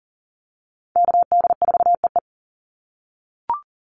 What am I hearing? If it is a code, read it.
Morse code, heard as a keyed tone, about 40 wpm, 700 Hz. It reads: KD4EE
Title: Call Signs - 40wpm